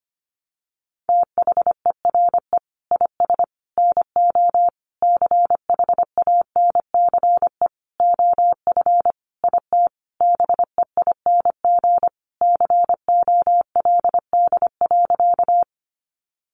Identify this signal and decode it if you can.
Morse code, heard as a keyed tone, about 25 wpm, 700 Hz. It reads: THERE IS NO CHANCE OF IT BEING COLD.
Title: Sentences from Top 800 Words 25wpm